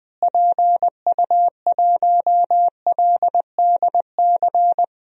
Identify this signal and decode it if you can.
Morse code, heard as a keyed tone, about 20 wpm, 700 Hz. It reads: PU1LDC
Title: DXCC-100 Call Signs - 6 Characters - Repeated 2x - 20wpm